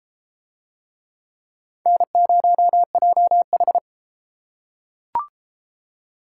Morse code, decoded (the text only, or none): N0JH